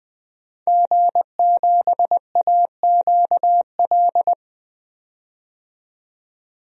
Morse code, heard as a keyed tone, about 20 wpm, 700 Hz. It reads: G7AQL
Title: DXCC-100 Call Signs - 5 Characters - 20wpm